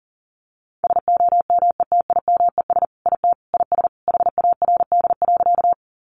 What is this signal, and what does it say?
Morse code, heard as a keyed tone, about 40 wpm, 700 Hz. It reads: SOMETIMES IT IS HARD.
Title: Sentences from Top 600 Words 40wpm